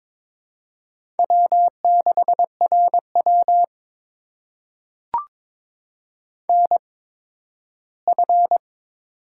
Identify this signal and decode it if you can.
Morse code, heard as a keyed tone, about 22 wpm, spaced as for 13 wpm, 700 Hz. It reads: W6RW NF
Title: USA Call Signs - 8x Character Spacing - 22wpm